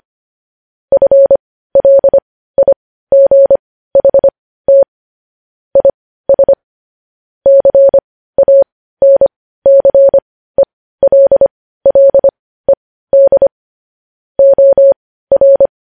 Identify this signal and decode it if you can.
Morse code, heard as a keyed tone, about 25 wpm, spaced as for 15 wpm, 550 Hz. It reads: FLIGHT IS CANCELLED OR
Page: news headlines in morse code at 15 wpm